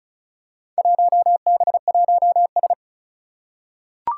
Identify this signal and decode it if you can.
Morse code, heard as a keyed tone, about 36 wpm, 700 Hz. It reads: 1B1S
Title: Four Challenging Letter-Number Combinations 35wpm